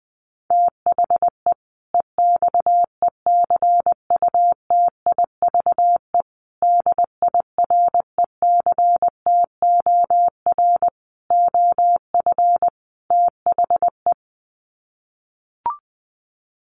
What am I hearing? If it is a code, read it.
Morse code, heard as a keyed tone, about 20 wpm, 700 Hz. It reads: THE EXECUTIVE DIRECTOR OF THE
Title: Sets of 5 Words Encore 20wpm